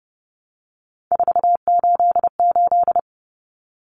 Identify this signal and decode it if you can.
Morse code, heard as a keyed tone, about 30 wpm, 700 Hz. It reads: 488
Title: Common Cut Numbers - 3-Digit 30wpm